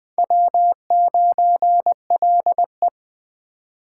Morse code, heard as a keyed tone, about 24 wpm, 700 Hz. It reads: W9LE
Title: Speed Racing - USA Call Signs 20wpm